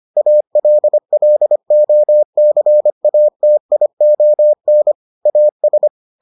Morse code, heard as a keyed tone, about 25 wpm, 600 Hz. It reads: ALLOCATION AS